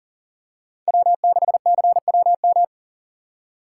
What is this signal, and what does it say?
Morse code, heard as a keyed tone, about 40 wpm, 700 Hz. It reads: W6CWM
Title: K1USN SST QSO Practice - 40wpm